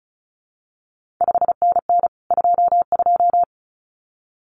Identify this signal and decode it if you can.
Morse code, heard as a keyed tone, about 36 wpm, 700 Hz. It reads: 5NN 22
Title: CQ WW DX Contest Exchange Encore - 35wpm